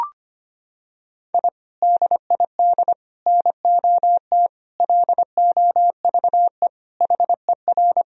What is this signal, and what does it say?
Morse code, heard as a keyed tone, about 25 wpm, 700 Hz. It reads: I DID NOT LOVE HER
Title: Sentences from Top 1000 Words - Review Part 2 - 25wpm